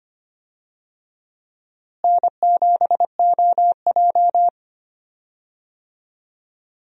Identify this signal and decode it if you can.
Morse code, heard as a keyed tone, about 25 wpm, 700 Hz. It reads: N7OJ
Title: USA Call Signs - 4 Characters - Repeated 2x - 25wpm